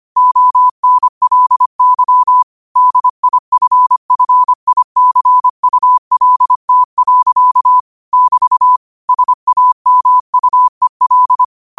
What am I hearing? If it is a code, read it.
Morse code, heard as a keyed tone, about 25 wpm, 1000 Hz. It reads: ONLY DIFFICULT. <BT> SAMUEL
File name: qotdfiles / 25wpm